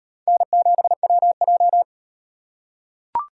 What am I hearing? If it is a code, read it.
Morse code, heard as a keyed tone, about 38 wpm, 700 Hz. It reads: N7WJ